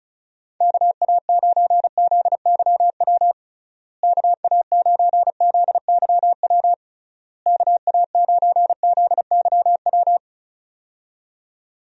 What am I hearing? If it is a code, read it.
Morse code, heard as a keyed tone, about 34 wpm, 700 Hz. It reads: KA9ZYW KA9ZYW KA9ZYW
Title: USA Call Signs - 6 Characters - Repeated 3x - 35wpm